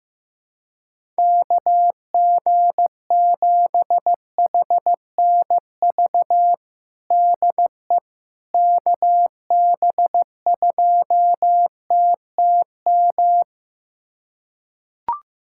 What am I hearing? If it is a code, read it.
Morse code, heard as a keyed tone, about 15 wpm, 700 Hz. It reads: KG7HNV DE KB2TTM
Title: USA Callsign DE Callsign - 6 Characters - 15wpm